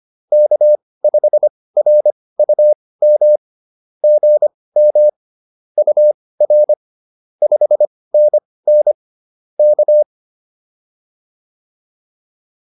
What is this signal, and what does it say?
Morse code, heard as a keyed tone, about 25 wpm, spaced as for 18 wpm, 600 Hz. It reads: K5RUM GM UR 5NN K
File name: POTA108 18